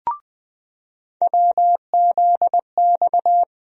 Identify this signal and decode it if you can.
Morse code, heard as a keyed tone, about 20 wpm, 700 Hz. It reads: WZX
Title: USA Call Sign Postfixes - 3 Letters 20wpm